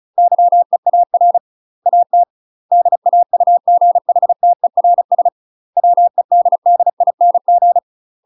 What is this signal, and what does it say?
Morse code, heard as a keyed tone, about 36 wpm, 700 Hz. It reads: YEAR AT DAUGHTERS WEDDING
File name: UK MASTER QSOs TRAINING / 35WPM / UK MASTER QSOs 0012